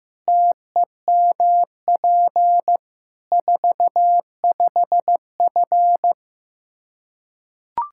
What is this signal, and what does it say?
Morse code, heard as a keyed tone, about 15 wpm, 700 Hz. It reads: TEMP 45F
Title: Practice QSO - Exchange 2 - Info - 15wpm